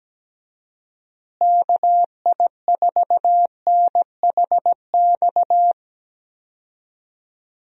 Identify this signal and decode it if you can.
Morse code, heard as a keyed tone, about 17 wpm, 700 Hz. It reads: KI4NHX
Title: USA Call Signs - 6 Characters - Repeated 2x - 17wpm